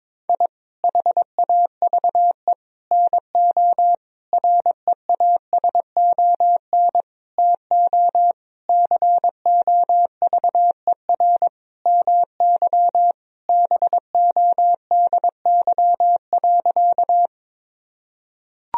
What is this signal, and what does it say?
Morse code, heard as a keyed tone, about 22 wpm, 700 Hz. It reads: I HAVE NO REASON TO COVER MY BODY.
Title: Sentences from Top 1000 Words - Review Part 1 - 22wpm